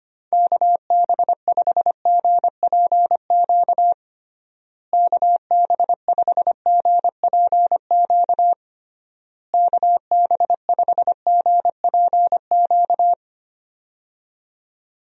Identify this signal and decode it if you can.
Morse code, heard as a keyed tone, about 25 wpm, 700 Hz. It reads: KB5GPQ KB5GPQ KB5GPQ
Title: USA Call Signs - 6 Characters - Repeated 3x - 25wpm